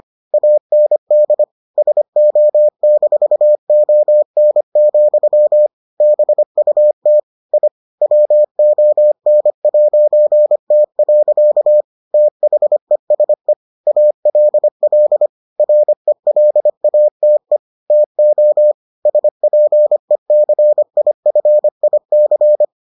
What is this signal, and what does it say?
Morse code, heard as a keyed tone, about 25 wpm, 600 Hz. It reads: AND SO-ON, BUT I WON'T. THESE ALL RELATE TO SPECIFIC